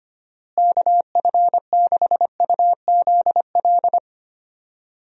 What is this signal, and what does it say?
Morse code, heard as a keyed tone, about 25 wpm, 700 Hz. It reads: KF6UZL
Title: USA Call Signs - 6 Characters - Repeated 2x - 25wpm